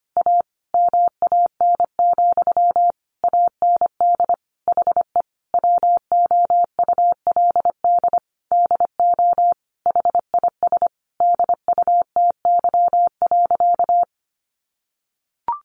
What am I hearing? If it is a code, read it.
Morse code, heard as a keyed tone, about 25 wpm, 700 Hz. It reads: A MAN, AND HE WOULD DO HIS DUTY.
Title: Sentences from Top 1500 Words 25wpm